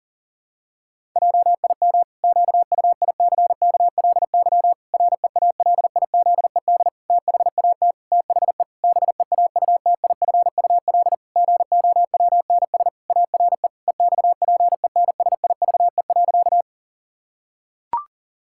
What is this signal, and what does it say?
Morse code, heard as a keyed tone, about 40 wpm, 700 Hz. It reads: JIM QUICKLY REALIZED THAT THE BEAUTIFUL GOWNS ARE EXPENSIVE.
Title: Pangram Challenge v2 - 40wpm